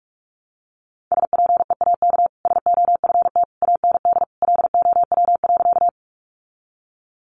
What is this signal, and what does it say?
Morse code, heard as a keyed tone, about 44 wpm, 700 Hz. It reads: SPEAK SOFT AND LOW.
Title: Sentences from Top 1500 Words 45wpm